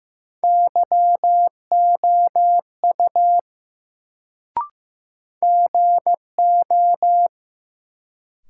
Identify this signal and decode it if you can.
Morse code, heard as a keyed tone, about 15 wpm, 700 Hz. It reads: YOU GO
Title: Sentences from Top 100 Words - New Word Drill - 15wpm